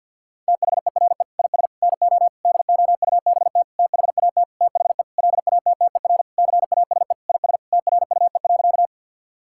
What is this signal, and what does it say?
Morse code, heard as a keyed tone, about 50 wpm, 700 Hz. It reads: THERE IS NO DOUBT THAT THE LATTER CASE IS TRUE.
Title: Sentences from Top 1600 Words 50wpm